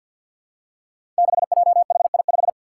Morse code, heard as a keyed tone, about 47 wpm, 700 Hz. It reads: 6JHI5
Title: Five Challenging Letter-Number Combinations 50wpm